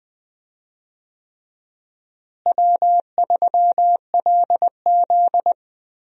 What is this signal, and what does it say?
Morse code, heard as a keyed tone, about 20 wpm, 700 Hz. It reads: W3LZ
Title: USA Call Signs - 4 Characters - Repeated 3x - 20wpm